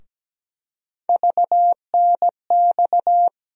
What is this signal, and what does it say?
Morse code, heard as a keyed tone, about 17 wpm, 700 Hz. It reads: VNX
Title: USA Call Sign Postfixes - 3 Letters 17wpm